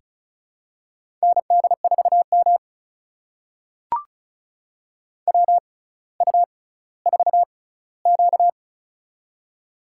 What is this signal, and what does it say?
Morse code, heard as a keyed tone, about 34 wpm, spaced as for 18 wpm, 700 Hz. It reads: ND4M WU4Q
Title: USA Call Signs - 4 Characters - 6x Character Spacing - 35wpm